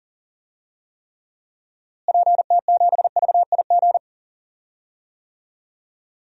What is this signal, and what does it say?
Morse code, heard as a keyed tone, about 40 wpm, 700 Hz. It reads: PT7VIG
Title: DXCC-100 Call Signs - 6 Characters - Repeated 3x - 40wpm